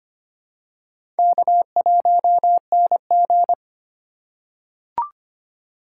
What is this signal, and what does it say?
Morse code, heard as a keyed tone, about 25 wpm, 700 Hz. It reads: K1NG
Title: USA Call Signs - 4 Characters - 2x Character Spacing - 25wpm